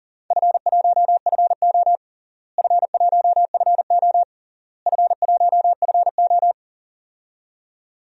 Morse code, heard as a keyed tone, about 37 wpm, 700 Hz. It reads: F1FO F1FO F1FO
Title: DXCC-100 Call Signs - 4 Characters - Repeated 3x - 40wpm